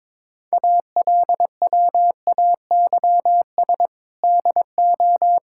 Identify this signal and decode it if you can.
Morse code, heard as a keyed tone, about 22 wpm, 700 Hz. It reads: ALWAYS DO
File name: practice / 22wpm / Sentences with 8-Word Phrases - 22wpm